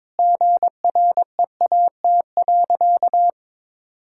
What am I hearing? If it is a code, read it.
Morse code, heard as a keyed tone, about 22 wpm, 700 Hz. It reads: GREAT.
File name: practice / 22wpm / Sentences with 2-Word Phrases - 22wpm